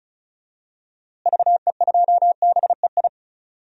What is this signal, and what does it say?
Morse code, heard as a keyed tone, about 34 wpm, 700 Hz. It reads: VE2BEI
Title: DXCC-100 Call Signs - 6 Characters - Repeated 3x - 35wpm